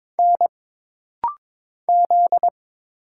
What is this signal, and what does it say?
Morse code, heard as a keyed tone, about 22 wpm, 700 Hz. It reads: N Z
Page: This rapid-fire timing is geared to push your ICR (Instant Character Recognition) skills.